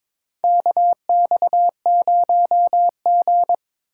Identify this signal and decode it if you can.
Morse code, heard as a keyed tone, about 22 wpm, 700 Hz. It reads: KX0G
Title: USA Call Signs - 4 Characters - 3x Character Spacing - 22wpm